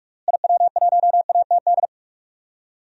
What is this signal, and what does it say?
Morse code, heard as a keyed tone, about 46 wpm, 700 Hz. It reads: IW1ATD
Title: DXCC-100 Call Signs - 6 Characters - Repeated 3x - 45wpm